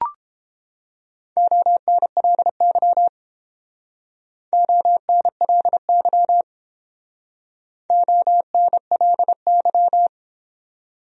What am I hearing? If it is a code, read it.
Morse code, heard as a keyed tone, about 31 wpm, 700 Hz. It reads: ONLY ONLY ONLY